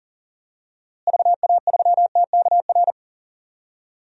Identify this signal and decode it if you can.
Morse code, heard as a keyed tone, about 40 wpm, 700 Hz. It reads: VA3TKR
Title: DXCC-100 Call Signs - 6 Characters - Repeated 2x - 40wpm